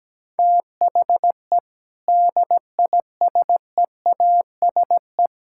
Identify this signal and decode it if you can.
Morse code, heard as a keyed tone, about 17 wpm, 700 Hz. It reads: THE DISEASE
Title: Sentences from Top 1000 Words - Review Part 6 - 17wpm